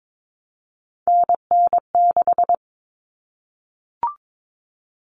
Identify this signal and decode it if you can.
Morse code, heard as a keyed tone, about 22 wpm, 700 Hz. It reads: NN6
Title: Common Cut Numbers - 3-Digit 22wpm